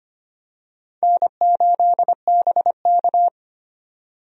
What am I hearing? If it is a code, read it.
Morse code, heard as a keyed tone, about 25 wpm, 700 Hz. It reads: N8BK
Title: USA Call Signs - 4 Characters - 25wpm